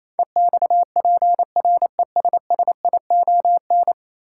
Morse code, heard as a keyed tone, about 28 wpm, 700 Hz. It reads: EXPRESSION
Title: Sentences from Top 1200 Words - New Word Drill Alt - 28wpm